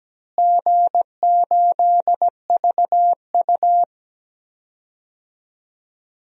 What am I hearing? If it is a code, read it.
Morse code, heard as a keyed tone, about 17 wpm, 700 Hz. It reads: G8VU
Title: DXCC-100 Call Signs - 4 Characters - Repeated 3x - 17wpm